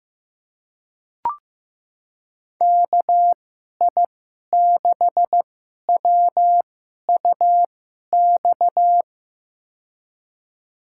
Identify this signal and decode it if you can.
Morse code, heard as a keyed tone, about 16 wpm, spaced as for 11 wpm, 700 Hz. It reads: KI6WUX
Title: USA Call Signs - 2x Character Spacing - 15wpm